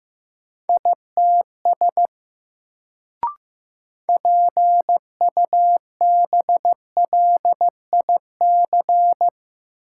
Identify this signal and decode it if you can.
Morse code, heard as a keyed tone, about 15 wpm, 700 Hz. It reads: ITS PUBLIC
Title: Sets of 2 Words - New Word Drill - 15wpm